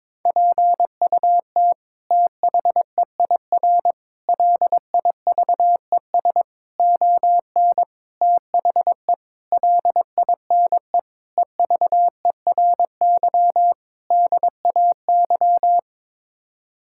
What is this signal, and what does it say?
Morse code, heard as a keyed tone, about 22 wpm, 700 Hz. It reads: PUT THEIR LIVES ON THE LINE EVERY DAY
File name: practice / 22wpm / Sets of 8 Words - 22wpm